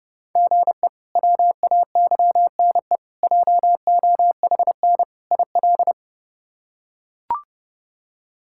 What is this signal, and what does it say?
Morse code, heard as a keyed tone, about 30 wpm, 700 Hz. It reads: GE WAYNE JOHN IL
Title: K1USN SST QSO Practice - 30wpm